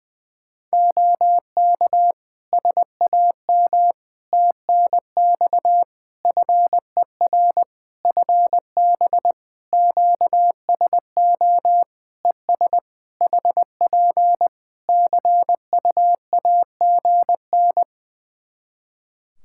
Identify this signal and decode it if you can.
Morse code, heard as a keyed tone, about 20 wpm, 700 Hz. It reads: OK SAM TNX FER FB QSO ES HP CUAGN
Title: Practice QSO - Exchange 4 - Ending - 20wpm